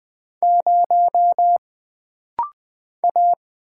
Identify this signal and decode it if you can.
Morse code, heard as a keyed tone, about 20 wpm, 700 Hz. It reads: 0 A
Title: Single Character (Letter:Number:Punctuation:Prosign) - Rapid-Fire - 20wpm